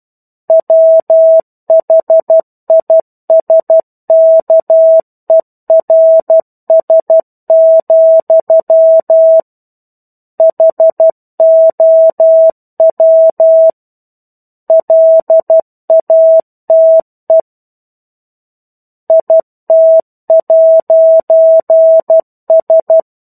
Morse code, heard as a keyed tone, about 12 wpm, 650 Hz. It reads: WHISKERS, HOW LATE IT'S
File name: 01-Chapter-01 12 WPM